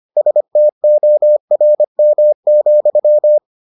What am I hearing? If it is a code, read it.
Morse code, heard as a keyed tone, about 25 wpm, 600 Hz. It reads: STORM,